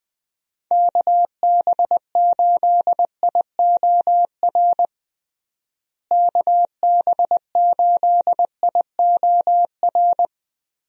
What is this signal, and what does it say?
Morse code, heard as a keyed tone, about 20 wpm, 700 Hz. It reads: KB8IOR KB8IOR
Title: USA Call Signs - 6 Characters - Repeated 3x - 20wpm